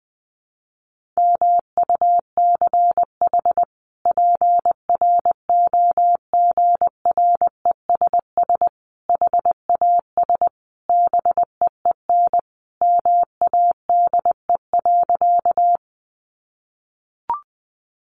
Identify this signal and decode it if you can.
Morse code, heard as a keyed tone, about 20 wpm, 700 Hz. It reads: MUCH PROGRESS HAS BEEN MADE.
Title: Sentences from Top 1400 Words 20wpm